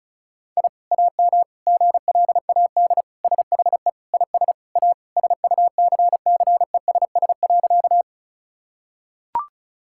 Morse code, heard as a keyed tone, about 36 wpm, 700 Hz. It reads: I AM GLAD SHE IS A SUCCESS.
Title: Sentences from Top 1000 Words - Review Part 6 - 35wpm